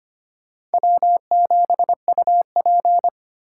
Morse code, heard as a keyed tone, about 32 wpm, 700 Hz. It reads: W7UP